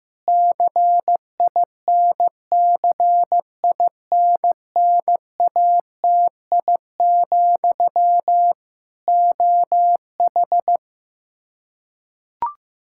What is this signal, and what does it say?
Morse code, heard as a keyed tone, about 15 wpm, 700 Hz. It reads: CINCINNATI, OH
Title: Practice QSO - Exchange 1 - Beginning - 15wpm